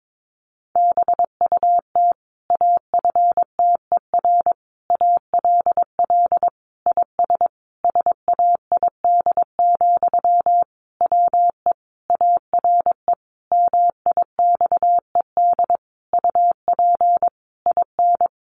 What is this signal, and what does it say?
Morse code, heard as a keyed tone, about 22 wpm, 700 Hz. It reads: BUT AFTER ALL IS SAID, WE ARE MIXED UP IN
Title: Sentences from Top 1500 Words 22wpm